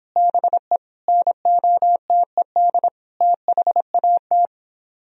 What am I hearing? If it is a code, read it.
Morse code, heard as a keyed tone, about 29 wpm, 700 Hz. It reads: BE NOTED THAT